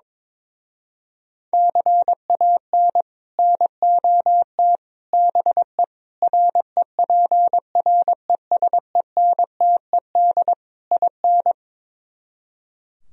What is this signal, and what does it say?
Morse code, heard as a keyed tone, about 22 wpm, 700 Hz. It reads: CAN NOT BE REPRESENTED IN
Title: Speed-Racing - Sets of 5 Words 22wpm